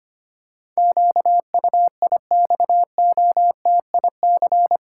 Morse code, heard as a keyed tone, about 25 wpm, 700 Hz. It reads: QUIXOTIC
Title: Pangram Challenge v2 - 25wpm